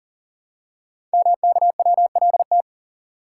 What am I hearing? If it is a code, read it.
Morse code, heard as a keyed tone, about 42 wpm, 700 Hz. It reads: MKWLT
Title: Code Groups - 5 Letter:Numbers - 40wpm